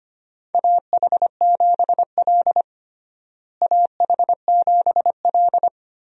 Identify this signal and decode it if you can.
Morse code, heard as a keyed tone, about 25 wpm, 700 Hz. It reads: AH7L AH7L
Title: USA Call Signs - 4 Characters - Repeated 3x - 25wpm